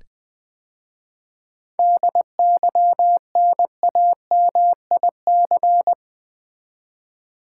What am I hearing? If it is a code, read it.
Morse code, heard as a keyed tone, about 20 wpm, 700 Hz. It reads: DYNAMIC
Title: 7 Letter Words - First 3 Letters Given - 20wpm